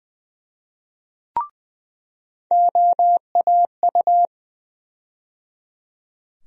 Morse code, heard as a keyed tone, about 20 wpm, 700 Hz. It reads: OAU
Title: USA Call Sign Postfixes - 3 Letters 20wpm